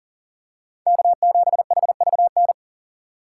Morse code, heard as a keyed tone, about 38 wpm, 700 Hz. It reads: K7HVN
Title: USA Call Signs - 5 Characters - 4x Character Spacing - 40wpm